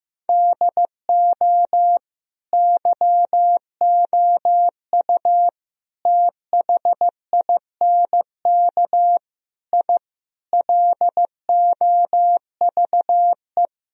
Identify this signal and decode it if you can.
Morse code, heard as a keyed tone, about 16 wpm, 700 Hz. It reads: DO YOU THINK I LOVE
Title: Sentences from Top 1000 Words - Review Part 2 - 15wpm